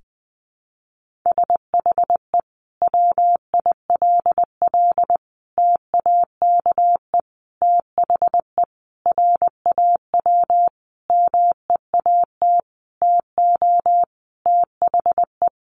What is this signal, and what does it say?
Morse code, heard as a keyed tone, about 20 wpm, 700 Hz. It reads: SHE WILL TAKE THE RAW MEAT TO THE
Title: Sentences from Top 1500 Words 20wpm